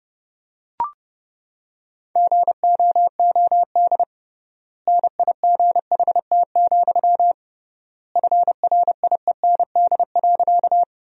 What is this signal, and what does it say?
Morse code, heard as a keyed tone, about 30 wpm, 700 Hz. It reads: GOOD NIGHT, FRIEND.
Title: 3x Word-Spacing - Sentences from Top 300 Words - 30wpm